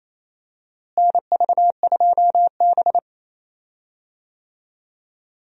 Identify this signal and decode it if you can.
Morse code, heard as a keyed tone, about 28 wpm, 700 Hz. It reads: NV2B
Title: USA Call Signs - 4 Characters - Repeated 2x - 28wpm